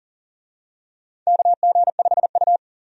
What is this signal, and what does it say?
Morse code, heard as a keyed tone, about 40 wpm, 700 Hz. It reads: KG5U
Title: USA Call Signs - 4 Characters - 40wpm